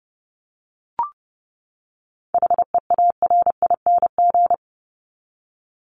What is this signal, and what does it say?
Morse code, heard as a keyed tone, about 30 wpm, 700 Hz. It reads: HEARING
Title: Top 1751-2000 Words 30wpm